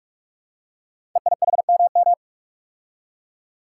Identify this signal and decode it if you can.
Morse code, heard as a keyed tone, about 44 wpm, 700 Hz. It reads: EIHMM
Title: Code Groups - 5 Letters - 45wpm